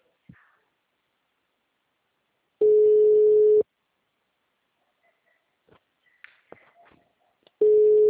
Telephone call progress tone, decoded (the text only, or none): ringback tone